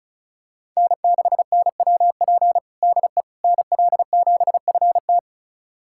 Morse code, heard as a keyed tone, about 36 wpm, 700 Hz. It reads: N6NWP DE NL7FT
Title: USA Callsign DE Callsign - 5 Characters - 35wpm